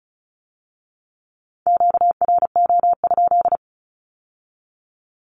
Morse code, heard as a keyed tone, about 36 wpm, 700 Hz. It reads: QRO?
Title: Q-Codes with Spoken Meaning - 35wpm